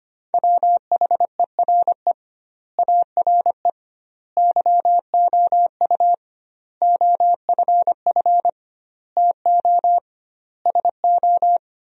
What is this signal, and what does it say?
Morse code, heard as a keyed tone, about 25 wpm, 700 Hz. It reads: WHERE ARE YOU OFF TO SO
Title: 2x Word-Spacing - Sentences from Top 400 Words - 25wpm